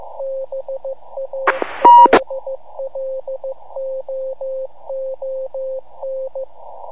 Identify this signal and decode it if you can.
Morse code, heard as a keyed tone, about 15 wpm, 530 Hz. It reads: BALLOON
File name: iHAB-3 / Repeater Recordings